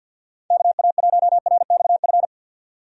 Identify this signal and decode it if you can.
Morse code, heard as a keyed tone, about 49 wpm, 700 Hz. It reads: KA1RXF
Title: USA Call Signs - 6 Characters - 50wpm